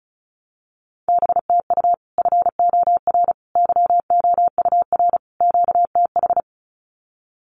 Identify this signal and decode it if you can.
Morse code, heard as a keyed tone, about 34 wpm, 700 Hz. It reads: BTU FOR YOUR QTH
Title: Common QSO Phrases 35wpm